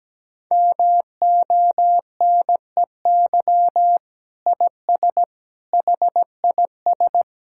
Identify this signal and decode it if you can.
Morse code, heard as a keyed tone, about 17 wpm, 700 Hz. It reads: MONEY IS HIS
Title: Sentences from Top 1600 Words 17wpm